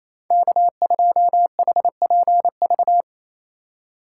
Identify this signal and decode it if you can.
Morse code, heard as a keyed tone, about 32 wpm, 700 Hz. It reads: K2HPV
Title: Speed Racing - USA 1x3 Call Signs 28wpm